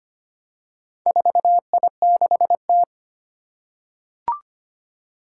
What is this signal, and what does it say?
Morse code, heard as a keyed tone, about 25 wpm, 700 Hz. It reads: 4I6T
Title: Four Challenging Letter-Number Combinations 25wpm